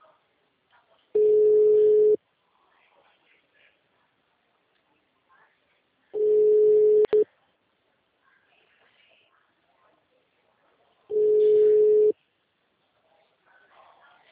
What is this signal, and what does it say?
Ringback tone (Europe, 425 Hz), with speech.